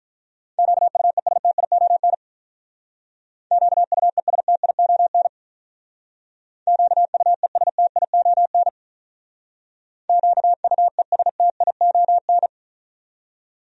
Morse code, heard as a keyed tone, about 43 wpm, 700 Hz. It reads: QUESTION QUESTION QUESTION QUESTION
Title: Speed-Racing Top 500 Words 35wpm